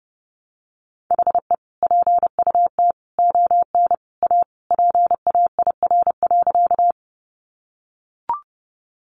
Morse code, heard as a keyed tone, about 30 wpm, 700 Hz. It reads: HE PUT ON A PAIR.
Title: Sentences from Top 1200 Words 30wpm